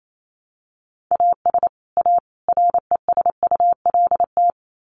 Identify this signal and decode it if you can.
Morse code, heard as a keyed tone, about 28 wpm, 700 Hz. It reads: AS A RESULT
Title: Sets of 3 Words 28wpm